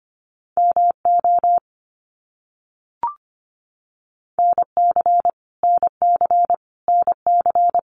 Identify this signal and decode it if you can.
Morse code, heard as a keyed tone, about 25 wpm, 700 Hz. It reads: MO NC NC NC
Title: US States Repeated 3x 25wpm